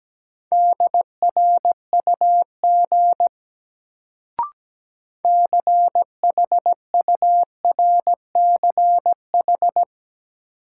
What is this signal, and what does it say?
Morse code, heard as a keyed tone, about 17 wpm, 700 Hz. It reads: DRUG CHURCH
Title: Sets of 5 Words - New Word Drill - 17wpm